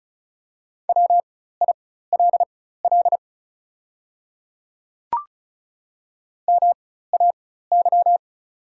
Morse code, heard as a keyed tone, about 36 wpm, spaced as for 16 wpm, 690 Hz. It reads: WILL MAY
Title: 4x Character Spacing - Top 300 Words - 35wpm